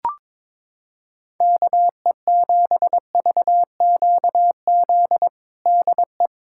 Morse code, heard as a keyed tone, about 22 wpm, 700 Hz. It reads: KE7VQZ DE
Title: USA Callsign DE Callsign - 6 Characters - 22wpm